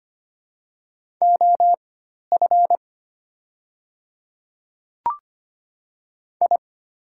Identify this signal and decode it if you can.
Morse code, heard as a keyed tone, about 25 wpm, spaced as for 12 wpm, 700 Hz. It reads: OF I
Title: 4x Character Spacing - Top 100 Words - 25wpm